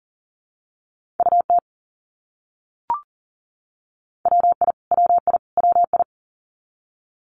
Morse code, heard as a keyed tone, about 40 wpm, 700 Hz. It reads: UT WI WI WI
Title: US States Repeated 3x 40wpm